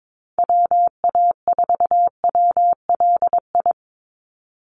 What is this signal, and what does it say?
Morse code, heard as a keyed tone, about 22 wpm, 700 Hz. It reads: WA4WLI
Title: Call Signs - 22wpm